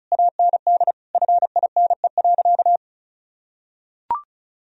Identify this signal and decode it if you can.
Morse code, heard as a keyed tone, about 36 wpm, 700 Hz. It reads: AND FINE.
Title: Sentences from Top 1000 Words - Review Part 2 - 35wpm